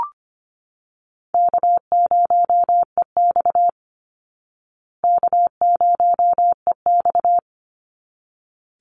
Morse code, heard as a keyed tone, about 25 wpm, 700 Hz. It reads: K0EX K0EX
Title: Easy Call Signs Repeated 2x - 25wpm